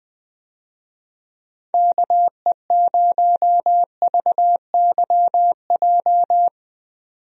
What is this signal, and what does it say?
Morse code, heard as a keyed tone, about 20 wpm, 700 Hz. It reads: KE0VYJ
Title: USA Call Signs - 6 Characters - 20wpm